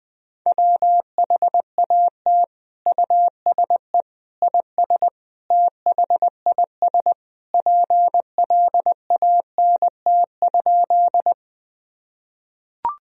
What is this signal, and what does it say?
Morse code, heard as a keyed tone, about 21 wpm, 700 Hz. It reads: WHAT USE IS THIS PLANT?
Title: Sentences from Top 1000 Words - Review Part 1 - 20wpm